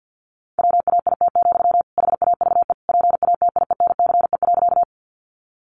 Morse code, heard as a keyed tone, about 50 wpm, 700 Hz. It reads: WAIT, HAVE PATIENCE.
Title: Sentences from Top 1300 Words 50wpm